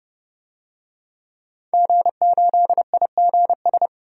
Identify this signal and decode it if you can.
Morse code, heard as a keyed tone, about 31 wpm, 700 Hz. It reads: G8IGS
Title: DXCC-100 Call Signs - 5 Characters - 30wpm